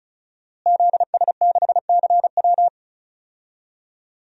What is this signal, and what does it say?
Morse code, heard as a keyed tone, about 36 wpm, 700 Hz. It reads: ZS6CW
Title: DXCC-100 Call Signs - 5 Characters - Repeated 2x - 35wpm